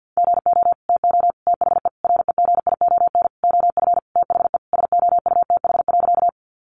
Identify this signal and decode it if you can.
Morse code, heard as a keyed tone, about 50 wpm, 700 Hz. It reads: GO TO THE REGION OF THE SOUTH.
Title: Sentences from Top 1500 Words 50wpm